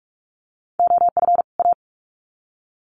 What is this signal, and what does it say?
Morse code, heard as a keyed tone, about 44 wpm, 700 Hz. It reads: OF A
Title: Sets of 2 Words 45wpm